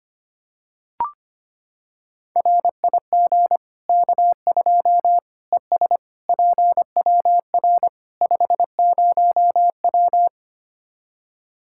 Morse code, heard as a keyed tone, about 25 wpm, 700 Hz. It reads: RIG K2 ES PWR 50W
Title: Practice QSO - Exchange 2 - Info - 25wpm